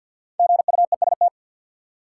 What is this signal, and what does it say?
Morse code, heard as a keyed tone, about 53 wpm, 700 Hz. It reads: GUEST
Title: Five Letter Words - 50wpm